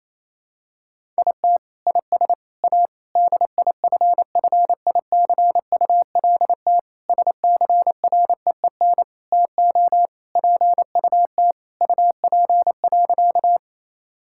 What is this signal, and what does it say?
Morse code, heard as a keyed tone, about 28 wpm, 700 Hz. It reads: IT IS A DIFFICULT SCREEN TO PUT UP.
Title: Sentences from Top 1000 Words - Review Part 6 - 28wpm